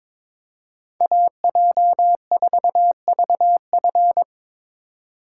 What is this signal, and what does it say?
Morse code, heard as a keyed tone, about 25 wpm, 700 Hz. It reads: AJ4VF
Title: Speed Racing - USA 2x2 Call Signs 22wpm